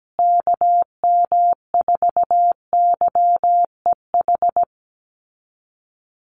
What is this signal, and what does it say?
Morse code, heard as a keyed tone, about 17 wpm, 700 Hz. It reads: KM4YEH
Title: USA Call Signs Repeated 2x 17wpm